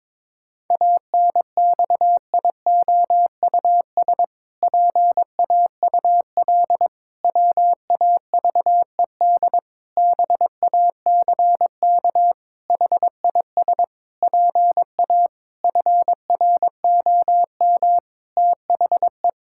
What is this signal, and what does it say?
Morse code, heard as a keyed tone, about 22 wpm, 700 Hz. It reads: ANXIOUS PAUL WAVED BACK HIS PA FROM THE
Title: Pangram Challenge v2 - 22wpm